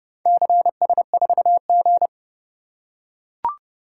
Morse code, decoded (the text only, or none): CS4G